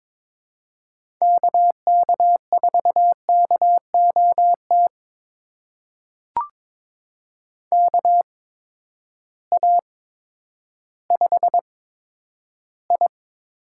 Morse code, heard as a keyed tone, about 22 wpm, spaced as for 11 wpm, 700 Hz. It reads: KK4KOT KA5I
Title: USA Call Signs - 6 Characters - 8x Character Spacing - 22wpm